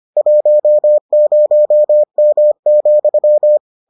Morse code, heard as a keyed tone, about 25 wpm, 600 Hz. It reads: 10M,